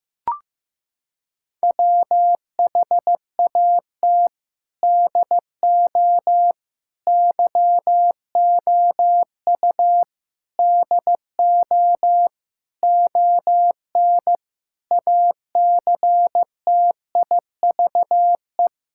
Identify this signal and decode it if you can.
Morse code, heard as a keyed tone, about 16 wpm, 700 Hz. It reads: WHAT DO YOU DO ON ACTIVE
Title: Sentences from Top 1000 Words - Review Part 6 - 15wpm